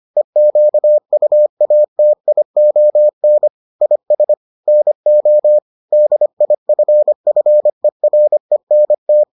Morse code, heard as a keyed tone, about 25 wpm, 600 Hz. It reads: EQUATION IS NO DIFFERENT